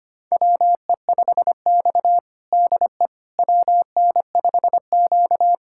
Morse code, heard as a keyed tone, about 25 wpm, 700 Hz. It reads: WE5X DE WN5Q
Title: USA Callsign DE Callsign - 4 Characters - 25wpm